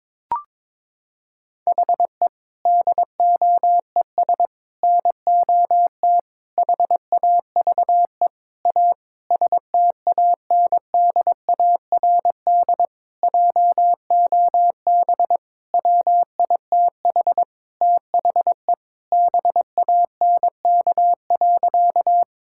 Morse code, heard as a keyed tone, about 22 wpm, 700 Hz. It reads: HE DOES NOT HAVE A STANDARD JOB WITH THE BANK.
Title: Sentences from Top 1000 Words - Review Part 6 - 22wpm